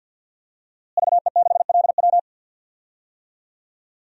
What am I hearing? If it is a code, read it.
Morse code, heard as a keyed tone, about 48 wpm, 700 Hz. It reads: VE6LW
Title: DXCC-100 Call Signs - 5 Characters - Repeated 3x - 50wpm